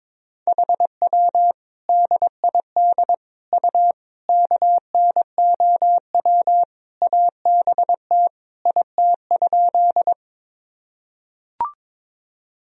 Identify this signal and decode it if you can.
Morse code, heard as a keyed tone, about 22 wpm, 700 Hz. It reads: HW DID U KNOW ABT IT?
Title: Sentences from Top 100 Words and Abbreviations 22wpm